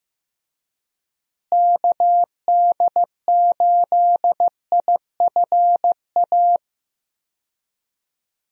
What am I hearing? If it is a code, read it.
Morse code, heard as a keyed tone, about 15 wpm, 700 Hz. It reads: KD8IFA
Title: USA Call Signs - 6 Characters - Repeated 3x - 15wpm